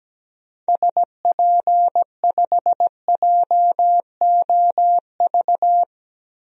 Morse code, heard as a keyed tone, about 17 wpm, 700 Hz. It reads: SP5JOV
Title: DXCC-100 Call Signs - 6 Characters - 17wpm